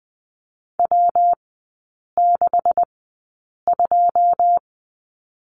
Morse code, heard as a keyed tone, about 20 wpm, spaced as for 8 wpm, 700 Hz. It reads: W62
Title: 127 Lesson 21e - Number 6 - Three Characters 20wpm